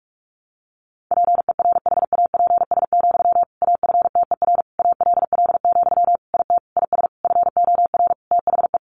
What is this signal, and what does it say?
Morse code, heard as a keyed tone, about 44 wpm, 700 Hz. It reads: PERHAPS, AFTER ALL, IT IS FOR THE
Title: Sentences from Top 500 Words 45wpm